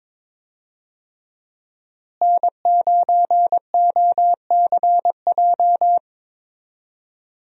Morse code, heard as a keyed tone, about 22 wpm, 700 Hz. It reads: N9OCJ
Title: USA Call Signs - 5 Characters - Repeated 3x - 22wpm